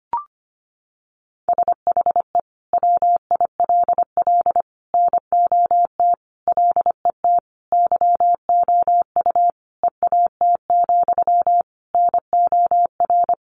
Morse code, heard as a keyed tone, about 25 wpm, 700 Hz. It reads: SHE WILL NOT LET YOU EAT, NOR
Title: Sentences from Top 1500 Words 25wpm